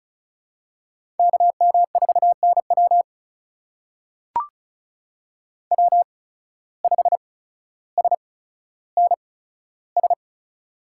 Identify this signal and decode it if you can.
Morse code, heard as a keyed tone, about 34 wpm, spaced as for 15 wpm, 700 Hz. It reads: KM4NW W5SNS
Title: USA Call Signs - 5 Characters - 8x Character Spacing - 35wpm